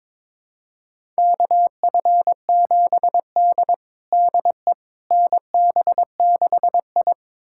Morse code, heard as a keyed tone, about 22 wpm, 700 Hz. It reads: KF7D DE NB6I
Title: USA Callsign DE Callsign - 4 Characters - 22wpm